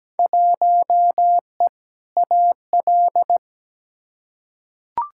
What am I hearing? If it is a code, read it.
Morse code, heard as a keyed tone, about 17 wpm, 700 Hz. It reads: 1E AL
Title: Field Day Exchanges - 17wpm